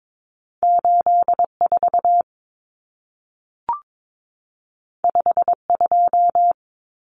Morse code, heard as a keyed tone, about 22 wpm, 700 Hz. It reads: 84 52
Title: Numbers - 2-Digit 22wpm